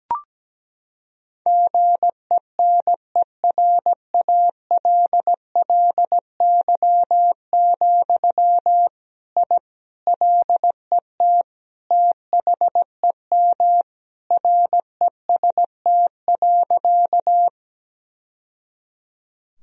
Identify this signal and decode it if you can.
Morse code, heard as a keyed tone, about 17 wpm, 700 Hz. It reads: GENERALLY, I LET THEM REST.
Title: Sentences from Top 1000 Words - Review Part 4 - 17wpm